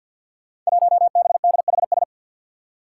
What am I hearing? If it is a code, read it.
Morse code, heard as a keyed tone, about 51 wpm, 700 Hz. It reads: 1BDHS
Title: Five Challenging Letter-Number Combinations 50wpm